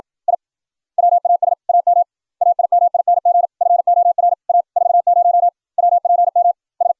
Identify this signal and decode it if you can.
Morse code, heard as a keyed tone, about 56 wpm, 700 Hz. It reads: I WAS AM AIMING FOR A 40 WPM S